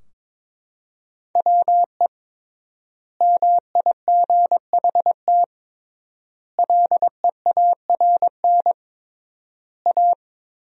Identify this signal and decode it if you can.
Morse code, heard as a keyed tone, about 22 wpm, 700 Hz. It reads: WE MIGHT LEARN A
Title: 3x Word-Spacing - Sentences from Top 300 Words - 22wpm